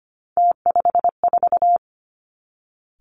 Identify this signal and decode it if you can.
Morse code, heard as a keyed tone, about 25 wpm, 700 Hz. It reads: T54
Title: Three Challenging Letter-Number Combinations 25wpm